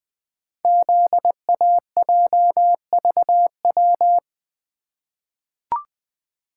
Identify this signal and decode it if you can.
Morse code, heard as a keyed tone, about 20 wpm, 700 Hz. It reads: ZAJVW
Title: Code Groups - 5 Letters - 20wpm